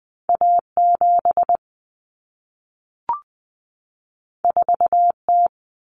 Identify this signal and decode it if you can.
Morse code, heard as a keyed tone, about 20 wpm, 700 Hz. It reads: A7 4T
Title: Common Cut Numbers - 2-Digit 20wpm